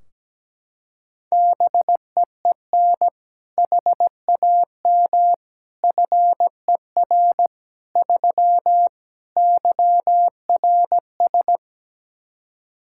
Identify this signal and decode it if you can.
Morse code, heard as a keyed tone, about 17 wpm, 700 Hz. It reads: BEEN HAM FER 3 YRS
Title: K1USN SST QSO Practice 17wpm